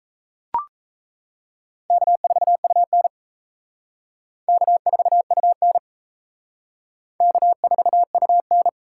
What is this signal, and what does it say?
Morse code, heard as a keyed tone, about 35 wpm, 700 Hz. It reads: K4UN K4UN K4UN